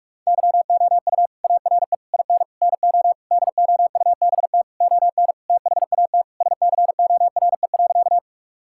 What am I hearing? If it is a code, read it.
Morse code, heard as a keyed tone, about 47 wpm, 700 Hz. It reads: YOU ARE IN NO DOUBT ON THAT SCORE.
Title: Sentences from Top 1600 Words 45wpm